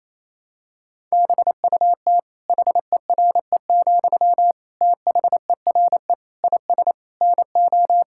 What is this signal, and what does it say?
Morse code, heard as a keyed tone, about 28 wpm, 700 Hz. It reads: BUT HERE, THERE IS NO
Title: Sentences from Top 1000 Words - Review Part 6 - 28wpm